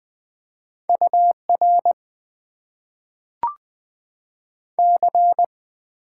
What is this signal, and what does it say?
Morse code, heard as a keyed tone, about 20 wpm, 700 Hz. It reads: UR C
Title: Top 90 QSO Elements - 8x Character Spacing - 20wpm